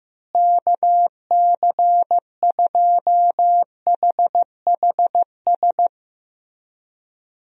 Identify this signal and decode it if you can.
Morse code, heard as a keyed tone, about 15 wpm, 700 Hz. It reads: KC2HHS
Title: USA Call Signs - 6 Characters - Repeated 2x - 15wpm